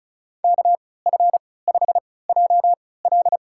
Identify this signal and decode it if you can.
Morse code, heard as a keyed tone, about 36 wpm, spaced as for 20 wpm, 700 Hz. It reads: KF5JL
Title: USA Call Signs - 5 Characters - 3x Character Spacing - 35wpm